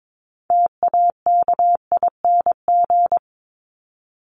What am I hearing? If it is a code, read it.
Morse code, heard as a keyed tone, about 22 wpm, 700 Hz. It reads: TAKING
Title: ING Suffix - Top 100 Words with Base Words 22wpm